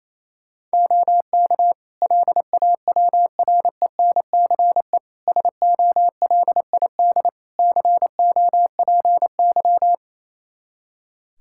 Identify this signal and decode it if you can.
Morse code, heard as a keyed tone, about 28 wpm, 700 Hz. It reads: OK LAWRENCE SOLID COPY
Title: Practice QSO - Exchange 3 - More Info - 28wpm